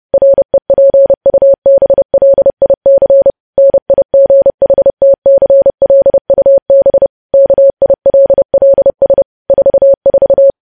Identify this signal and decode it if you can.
Morse code, heard as a keyed tone, about 30 wpm, 550 Hz. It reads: REPUBLIC NIGHTCLUB KILLS 44
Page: news headlines in morse code at 30 wpm